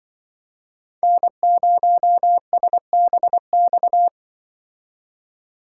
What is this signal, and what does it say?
Morse code, heard as a keyed tone, about 24 wpm, 700 Hz. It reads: N0SBX
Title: Speed Racing - USA Call Signs 20wpm